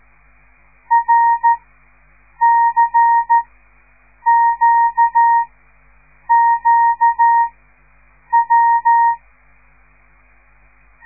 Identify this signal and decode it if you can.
Morse code, heard as a keyed tone, about 13 wpm, spaced as for 7 wpm, 930 Hz. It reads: RCQQW
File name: radiomorningstar / morse / les 5